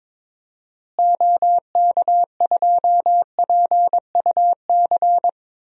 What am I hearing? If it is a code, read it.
Morse code, heard as a keyed tone, about 22 wpm, 700 Hz. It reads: OK2PUC
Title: DXCC-100 Call Signs - 6 Characters - Repeated 3x - 22wpm